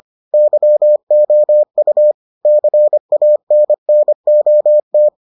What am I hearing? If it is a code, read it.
Morse code, heard as a keyed tone, about 25 wpm, 600 Hz. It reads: YOU CANNOT